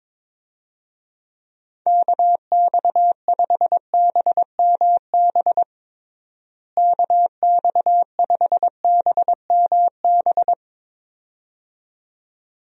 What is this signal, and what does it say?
Morse code, heard as a keyed tone, about 22 wpm, 700 Hz. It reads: KX5BMB KX5BMB
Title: USA Call Signs - 6 Characters - Repeated 2x - 22wpm